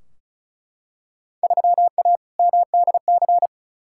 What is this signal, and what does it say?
Morse code, heard as a keyed tone, about 36 wpm, 700 Hz. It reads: 3A MDC
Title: Field Day Exchanges - 35wpm